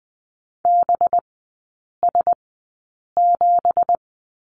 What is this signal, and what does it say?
Morse code, heard as a keyed tone, about 20 wpm, spaced as for 8 wpm, 690 Hz. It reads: BS7
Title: 205 Lesson 33e - Letter j - Three Characters 20wpm